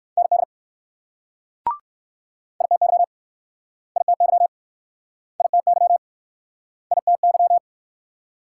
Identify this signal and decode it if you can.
Morse code, heard as a keyed tone, about 55 wpm, 690 Hz. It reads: ND ITY ITY ITY ITY
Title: Speed-Racing Multi-letter Phonemes - 45wpm